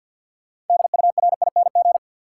Morse code, heard as a keyed tone, about 49 wpm, 700 Hz. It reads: DURING
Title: Sets of 2 Words - New Word Drill - 50wpm